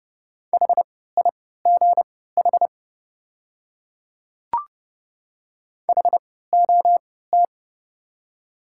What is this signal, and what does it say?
Morse code, heard as a keyed tone, about 32 wpm, spaced as for 17 wpm, 700 Hz. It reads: HIGH HOT
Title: 3x Character Spacing - Top 200 Words - 30wpm